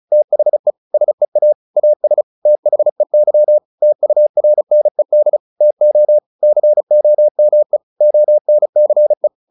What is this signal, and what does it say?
Morse code, heard as a keyed tone, about 34 wpm, 600 Hz. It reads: THE SEA AS THEY TURNED TO COME ONCE
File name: the old man and the sea 2 35wpm